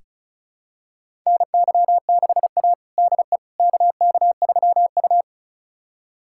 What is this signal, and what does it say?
Morse code, heard as a keyed tone, about 34 wpm, 700 Hz. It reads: NY6A DE KK3U
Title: USA Callsign DE Callsign - 4 Characters - 35wpm